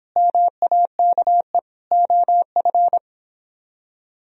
Morse code, heard as a keyed tone, about 26 wpm, 700 Hz. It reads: MAKE OF